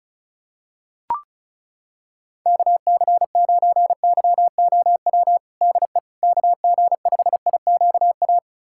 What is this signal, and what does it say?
Morse code, heard as a keyed tone, about 34 wpm, 700 Hz. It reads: KC9YOW DE KG5IQA
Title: USA Callsign DE Callsign - 6 Characters - 35wpm